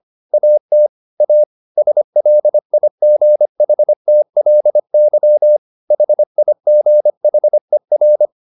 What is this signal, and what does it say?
Morse code, heard as a keyed tone, about 25 wpm, 600 Hz. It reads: AT A SLIGHTLY HIGHER
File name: the old man and the sea part 1 25wpm